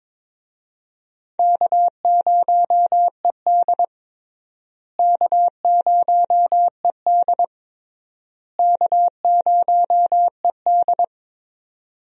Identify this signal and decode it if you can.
Morse code, heard as a keyed tone, about 22 wpm, 700 Hz. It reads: K0ED K0ED K0ED
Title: USA Call Signs - 4 Characters - Repeated 3x - 22wpm